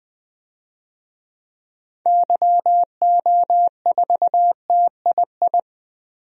Morse code, heard as a keyed tone, about 20 wpm, 700 Hz. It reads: YO4TII
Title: DXCC-100 Call Signs - 6 Characters - 20wpm